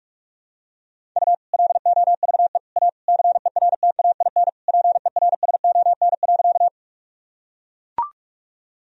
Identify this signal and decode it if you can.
Morse code, heard as a keyed tone, about 41 wpm, 700 Hz. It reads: U LOVE A CERTAIN PERSON.
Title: Sentences from Top 500 Words and Abbreviations 45wpm